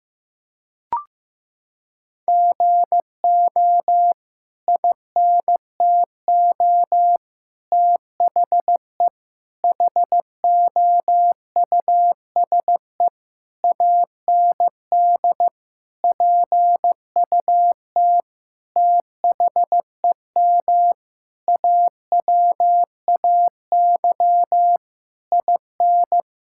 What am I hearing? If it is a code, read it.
Morse code, heard as a keyed tone, about 16 wpm, 700 Hz. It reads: GO INTO THE HOUSE AND PUT THEM AWAY IN
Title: Sentences from Top 1000 Words - Review Part 5 - 15wpm